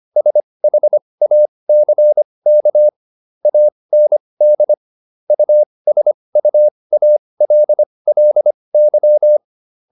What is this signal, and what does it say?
Morse code, heard as a keyed tone, about 25 wpm, spaced as for 20 wpm, 600 Hz. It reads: SHACK AND USUALLY